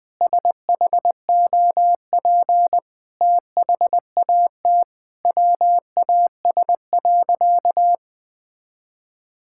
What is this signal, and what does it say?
Morse code, heard as a keyed tone, about 20 wpm, 700 Hz. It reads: SHOP THAT WAS.
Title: Sentences from Top 1600 Words 20wpm